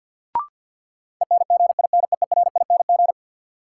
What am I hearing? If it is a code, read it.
Morse code, heard as a keyed tone, about 53 wpm, 700 Hz. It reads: ENGINEERING
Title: Sentences from Top 1500 Words - New Word Drill - 50wpm